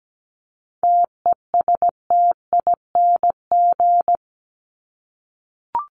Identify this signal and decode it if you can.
Morse code, heard as a keyed tone, about 17 wpm, 700 Hz. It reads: TESTING
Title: ING Suffix - Top 100 Words with Base Words 17wpm